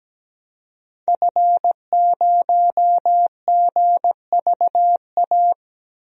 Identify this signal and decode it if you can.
Morse code, heard as a keyed tone, about 17 wpm, 700 Hz. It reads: F0GVA
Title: DXCC-100 Call Signs - 5 Characters - 17wpm